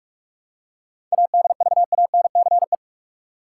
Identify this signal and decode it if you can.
Morse code, heard as a keyed tone, about 41 wpm, 700 Hz. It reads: ADVANCE
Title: Seven Letter Words - 45wpm